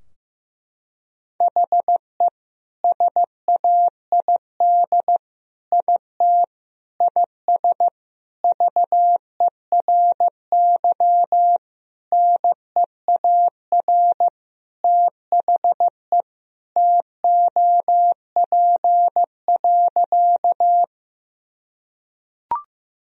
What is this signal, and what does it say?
Morse code, heard as a keyed tone, about 16 wpm, 700 Hz. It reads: HE SAID IT IS VERY NEAR THE TOP.
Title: Sentences from Top 1000 Words - Review Part 2 - 15wpm